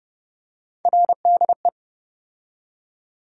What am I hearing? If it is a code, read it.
Morse code, heard as a keyed tone, about 30 wpm, 700 Hz. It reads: RDE
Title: USA Call Sign Postfixes - 3 Letters 30wpm